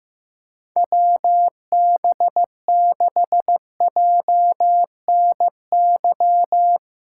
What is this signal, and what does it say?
Morse code, heard as a keyed tone, about 15 wpm, 700 Hz. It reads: WB6JNY
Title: USA Callsign DE Callsign 15wpm